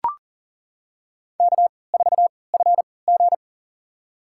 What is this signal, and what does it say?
Morse code, heard as a keyed tone, about 40 wpm, spaced as for 23 wpm, 700 Hz. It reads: K4FG
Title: USA Call Signs - 4 Characters - 3x Character Spacing - 40wpm